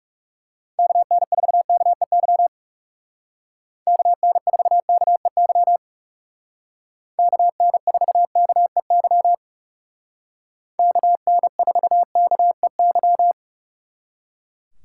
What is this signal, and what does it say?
Morse code, heard as a keyed tone, about 39 wpm, 700 Hz. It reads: KN4KEY KN4KEY KN4KEY KN4KEY